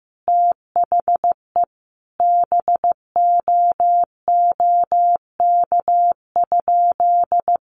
Morse code, heard as a keyed tone, about 15 wpm, 700 Hz. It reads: THE BOOK?
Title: Sentences from Top 1200 Words 15wpm